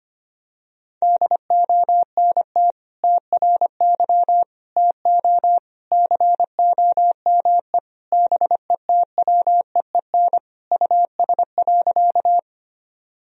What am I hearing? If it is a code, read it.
Morse code, heard as a keyed tone, about 25 wpm, 700 Hz. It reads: DONT TRY TO COME BETWEEN US.
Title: Sentences from Top 1000 Words - Review Part 1 - 25wpm